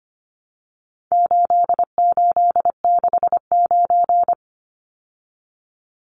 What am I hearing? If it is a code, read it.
Morse code, heard as a keyed tone, about 25 wpm, 700 Hz. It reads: 8869
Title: Numbers - 4-Digit 25wpm